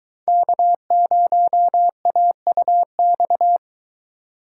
Morse code, heard as a keyed tone, about 22 wpm, 700 Hz. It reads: K0AUX